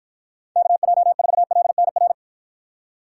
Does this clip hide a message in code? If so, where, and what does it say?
Yes: Morse code from 0.6 to 2.1 s, heard as a keyed tone, about 48 wpm, 700 Hz. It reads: KJ4LNR